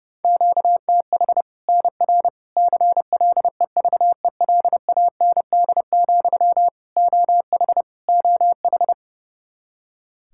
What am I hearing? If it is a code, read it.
Morse code, heard as a keyed tone, about 30 wpm, 700 Hz. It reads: QTH NR CLEVELAND, OH OH
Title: Practice QSO - Exchange 1 - Beginning - 30wpm